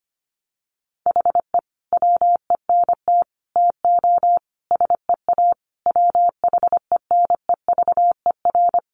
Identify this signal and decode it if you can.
Morse code, heard as a keyed tone, about 25 wpm, 700 Hz. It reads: HE WENT TO SEA WHENEVER
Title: Sentences from Top 1200 Words 25wpm